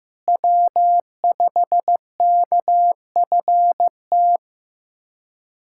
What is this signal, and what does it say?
Morse code, heard as a keyed tone, about 15 wpm, 700 Hz. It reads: W5KFT
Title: K1USN SST QSO Practice - 15wpm